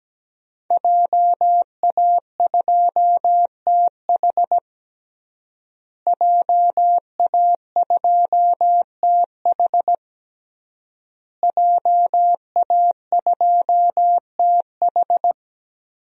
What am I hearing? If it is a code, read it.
Morse code, heard as a keyed tone, about 17 wpm, 700 Hz. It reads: JA2TH JA2TH JA2TH
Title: DXCC-100 Call Signs - 5 Characters - Repeated 3x - 17wpm